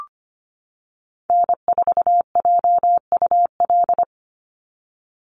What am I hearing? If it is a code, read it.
Morse code, heard as a keyed tone, about 25 wpm, 700 Hz. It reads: N4JUL
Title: USA Call Signs 25wpm